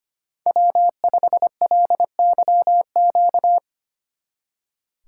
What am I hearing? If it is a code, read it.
Morse code, heard as a keyed tone, about 29 wpm, 700 Hz. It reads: W5LYQ
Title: Speed Racing - USA Call Signs 25wpm